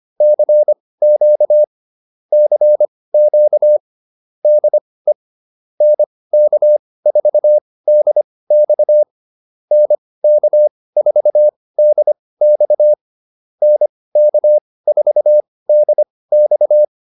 Morse code, heard as a keyed tone, about 25 wpm, spaced as for 18 wpm, 600 Hz. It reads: CQ CQ DE NK4DX NK4DX NK4DX
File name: QSO110 18